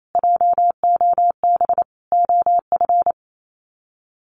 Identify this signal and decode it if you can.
Morse code, heard as a keyed tone, about 28 wpm, 700 Hz. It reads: JOB OF
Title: Sets of 5 Words Encore2 28wpm